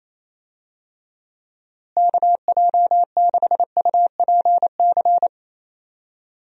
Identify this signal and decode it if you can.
Morse code, heard as a keyed tone, about 28 wpm, 700 Hz. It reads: KJ6UPC
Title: USA Call Signs - 6 Characters - Repeated 2x - 28wpm